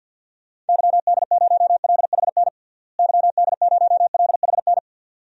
Morse code, heard as a keyed tone, about 51 wpm, 700 Hz. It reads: YD0LHN YD0LHN
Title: DXCC-100 Call Signs - 6 Characters - Repeated 2x - 50wpm